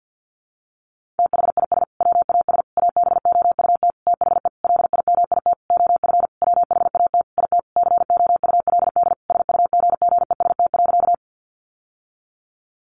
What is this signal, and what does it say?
Morse code, heard as a keyed tone, about 50 wpm, 700 Hz. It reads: THIS WAS ABOUT THE LIMIT OF WHAT IT COULD SUGGEST.
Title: Sentences from Top 1500 Words 50wpm